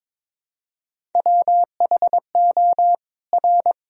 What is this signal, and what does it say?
Morse code, heard as a keyed tone, about 22 wpm, 700 Hz. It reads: WHO R
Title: Sentences from Top 200 Words and Abbreviations 22wpm